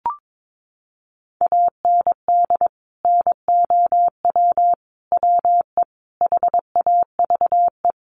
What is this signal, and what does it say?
Morse code, heard as a keyed tone, about 22 wpm, 700 Hz. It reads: AND NOW WE HAVE
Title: Sentences from Top 1500 Words 22wpm